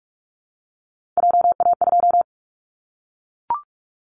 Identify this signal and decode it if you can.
Morse code, heard as a keyed tone, about 44 wpm, 700 Hz. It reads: JA2
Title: Callsign Prefixes for Top 25 DXCC - Repeated 3x - 45wpm